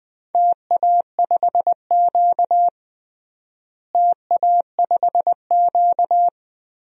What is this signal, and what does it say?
Morse code, heard as a keyed tone, about 21 wpm, 700 Hz. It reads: TA5Q TA5Q
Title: DXCC-100 Call Signs - 4 Characters - Repeated 2x - 20wpm